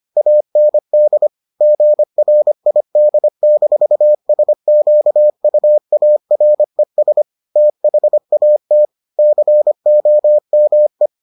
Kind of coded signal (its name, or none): Morse code